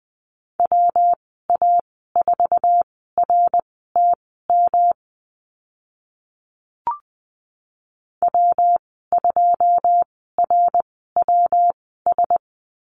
Morse code, heard as a keyed tone, about 20 wpm, spaced as for 14 wpm, 700 Hz. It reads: WA4RTM W2RWS
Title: 161 Lesson 26g - Letter g - Call Signs 20wpm